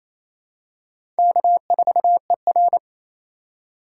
Morse code, heard as a keyed tone, about 27 wpm, 700 Hz. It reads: K4ER
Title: USA Call Signs - 4 Characters - 4x Character Spacing - 28wpm